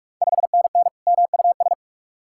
Morse code, heard as a keyed tone, about 46 wpm, 700 Hz. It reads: 5NN MUS
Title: Ontario QSO Party Exchange 45wpm